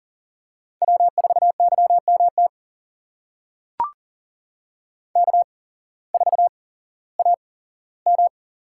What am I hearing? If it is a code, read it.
Morse code, heard as a keyed tone, about 40 wpm, spaced as for 19 wpm, 700 Hz. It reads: W4YMT K4AM
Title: USA Call Signs - 5 Characters - 8x Character Spacing - 40wpm